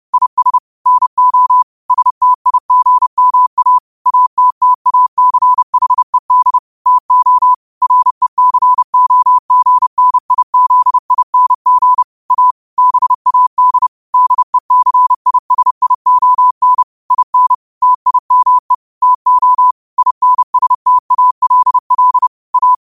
Morse code, heard as a keyed tone, about 30 wpm, 1000 Hz. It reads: IS NO STIGMA ATTACHED TO RECOGNIZING A BAD DECISION IN TIME TO INSTALL A
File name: qotdfiles / 30wpm